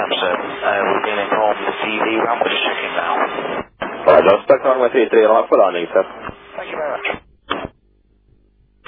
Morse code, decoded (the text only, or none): none